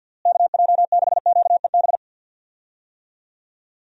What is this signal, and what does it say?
Morse code, heard as a keyed tone, about 51 wpm, 700 Hz. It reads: KJ6QEB
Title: USA Call Signs - 6 Characters - Repeated 2x - 50wpm